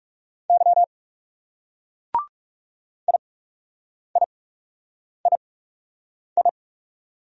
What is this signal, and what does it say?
Morse code, heard as a keyed tone, about 37 wpm, spaced as for 8 wpm, 700 Hz. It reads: Y IIII
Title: Speed-Racing Single Letters 30wpm